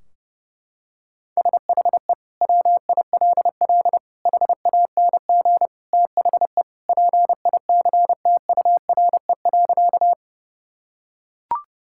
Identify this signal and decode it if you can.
Morse code, heard as a keyed tone, about 32 wpm, 700 Hz. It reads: SHE WILL HANG THE PICTURE.
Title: Sentences from Top 1600 Words 30wpm